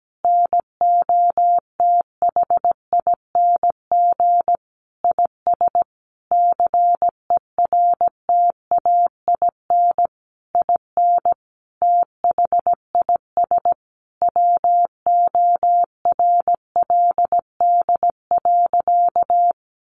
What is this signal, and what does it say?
Morse code, heard as a keyed tone, about 17 wpm, 700 Hz. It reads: NOTHING IS CERTAIN IN THIS WORLD.
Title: Sentences from Top 600 Words 17wpm